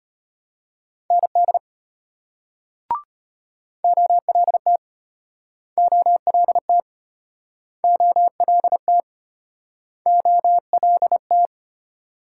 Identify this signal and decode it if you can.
Morse code, heard as a keyed tone, about 34 wpm, 700 Hz. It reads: ND OLT OLT OLT OLT